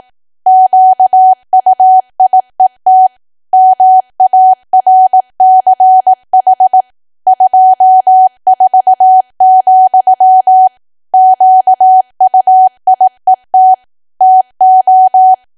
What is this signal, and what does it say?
Morse code, heard as a keyed tone, about 18 wpm, 750 Hz. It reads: QUIET MARCH 24, QUIET TO
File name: cwa / W1AW-Level2 / 18WPM